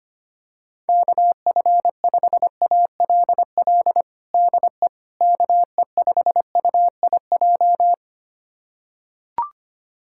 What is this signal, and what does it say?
Morse code, heard as a keyed tone, about 25 wpm, 700 Hz. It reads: KF5ALL DE KE5UIJ
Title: USA Callsign DE Callsign 25wpm